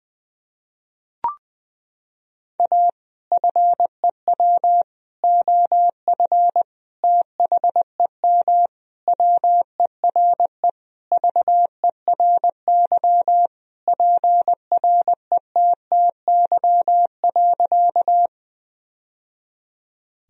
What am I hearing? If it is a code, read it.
Morse code, heard as a keyed tone, about 20 wpm, 700 Hz. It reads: A FEW OF THEM WERE VERY PRETTY.
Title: Sentences from Top 1000 Words - Review Part 1 - 20wpm